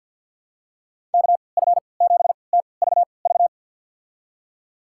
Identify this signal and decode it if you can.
Morse code, heard as a keyed tone, about 51 wpm, spaced as for 28 wpm, 700 Hz. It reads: KF7TVV
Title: USA Call Signs - 3x Character Spacing - 50wpm